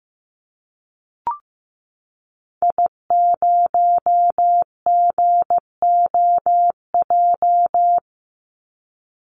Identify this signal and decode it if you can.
Morse code, heard as a keyed tone, about 15 wpm, 700 Hz. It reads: I0GOJ
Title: Call Signs Repeated 2x - 15wpm